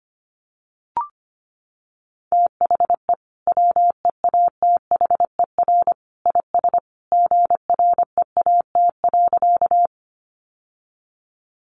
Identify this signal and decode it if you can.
Morse code, heard as a keyed tone, about 25 wpm, 700 Hz. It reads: THE WEATHER IS GREAT.
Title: Sentences from Top 800 Words 25wpm